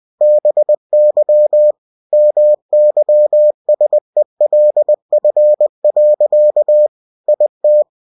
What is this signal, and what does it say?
Morse code, heard as a keyed tone, about 20 wpm, 600 Hz. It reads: BY MYSELF. IT
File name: cw-books / winnie-the-pooh / winnie-20wpm-page-0065